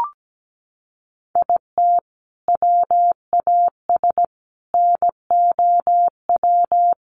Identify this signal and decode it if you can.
Morse code, heard as a keyed tone, about 17 wpm, 700 Hz. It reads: IT WAS NOW
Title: Sentences from Top 200 Words 17wpm